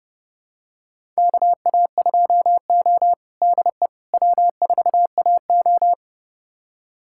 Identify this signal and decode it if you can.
Morse code, heard as a keyed tone, about 30 wpm, 700 Hz. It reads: KA2O DE W4AO
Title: USA Callsign DE Callsign - 4 Characters - 30wpm